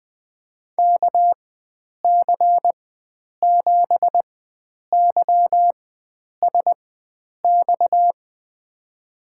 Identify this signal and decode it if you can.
Morse code, heard as a keyed tone, about 20 wpm, spaced as for 9 wpm, 700 Hz. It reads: KC7YSX
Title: USA Call Signs - 6 Characters - 4x Character Spacing - 20wpm